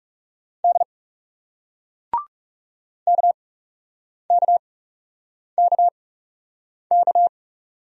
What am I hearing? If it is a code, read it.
Morse code, heard as a keyed tone, about 41 wpm, spaced as for 8 wpm, 700 Hz. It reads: D KKKK